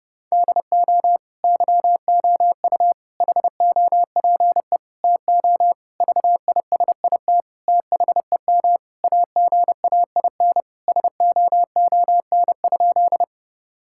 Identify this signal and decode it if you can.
Morse code, heard as a keyed tone, about 32 wpm, 700 Hz. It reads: DO YOU HOPE TO VISIT THEM AGAIN SOON?
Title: Sentences from Top 1000 Words - Review Part 6 - 30wpm